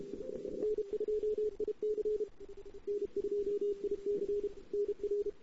Morse code, heard as a keyed tone, about 31 wpm, 410 Hz. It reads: N2IC N2IC NR
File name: CW / 2008-ss-cw-k5tr-op-n5rz -08:00